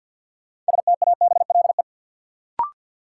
Morse code, heard as a keyed tone, about 53 wpm, 700 Hz. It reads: STABLE
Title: Sentences from Top 1500 Words - New Word Drill Alt - 50wpm